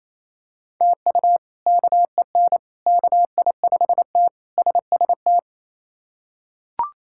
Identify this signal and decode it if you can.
Morse code, heard as a keyed tone, about 28 wpm, 700 Hz. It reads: TU KEN KI5T SST
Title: K1USN SST QSO Practice - 28wpm